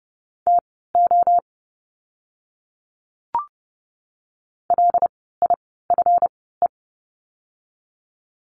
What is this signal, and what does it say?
Morse code, heard as a keyed tone, about 30 wpm, spaced as for 17 wpm, 700 Hz. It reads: TO LIFE
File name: course / 30:10wpm / 230 Lesson 37f - Letter x - Words 30wpm